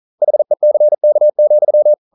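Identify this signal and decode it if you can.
Morse code, heard as a keyed tone, about 42 wpm, 600 Hz. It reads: HECK,
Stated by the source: Morse code practice at 41 wpm, from CW med Gnister.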